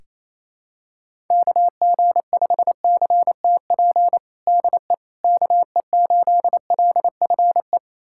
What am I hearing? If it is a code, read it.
Morse code, heard as a keyed tone, about 28 wpm, 700 Hz. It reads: KG5CTP DE KE8LFE
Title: USA Callsign DE Callsign - 6 Characters - 28wpm